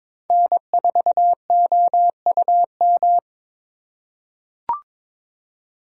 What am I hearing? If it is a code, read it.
Morse code, heard as a keyed tone, about 21 wpm, 700 Hz. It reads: N4OUM
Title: USA Call Signs - 4x Character Spacing - 22wpm